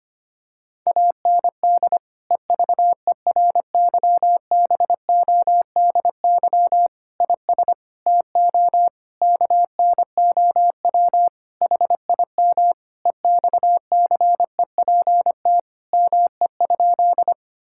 Morse code, heard as a keyed tone, about 25 wpm, 700 Hz. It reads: AND EVERYBODY IS TO KNOW HIM EXCEPT ME?
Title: Sentences from Top 1600 Words 25wpm